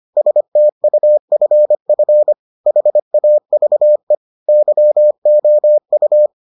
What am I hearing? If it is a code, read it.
Morse code, heard as a keyed tone, about 25 wpm, 600 Hz. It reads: STUFF HAVE YOU